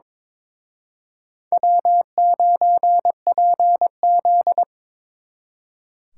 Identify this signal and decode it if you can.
Morse code, heard as a keyed tone, about 26 wpm, 700 Hz. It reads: W9PZ
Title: Speed Racing - USA 1x2 Call Signs 22wpm